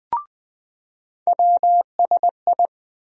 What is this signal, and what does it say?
Morse code, heard as a keyed tone, about 20 wpm, 700 Hz. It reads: WSI
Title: USA Call Sign Postfixes - 3 Letters 20wpm